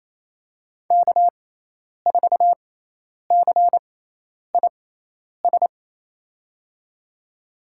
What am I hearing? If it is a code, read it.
Morse code, heard as a keyed tone, about 28 wpm, spaced as for 10 wpm, 700 Hz. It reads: K4CIS
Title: USA Call Signs - 5 Characters - 6x Character Spacing - 28wpm